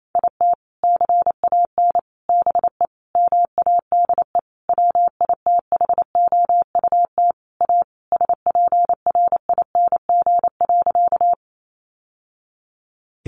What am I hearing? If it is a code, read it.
Morse code, heard as a keyed tone, about 28 wpm, 700 Hz. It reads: IT CAN BE MADE WITHOUT A SPRING.
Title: Sentences from Top 1200 Words 28wpm